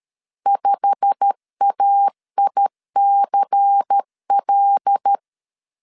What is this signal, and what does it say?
Morse code, heard as a keyed tone, about 13 wpm, 800 Hz. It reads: 5AICL
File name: LEZIONI / 10 SIGNS / 10D COMA 12wpm 2544